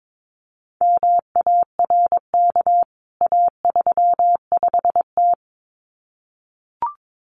Morse code, heard as a keyed tone, about 22 wpm, 700 Hz. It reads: MARK A35T
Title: CWT Exchange 22wpm